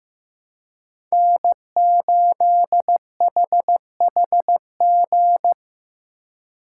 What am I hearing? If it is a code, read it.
Morse code, heard as a keyed tone, about 15 wpm, 700 Hz. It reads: N8HHG
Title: K1USN SST QSO Practice - 15wpm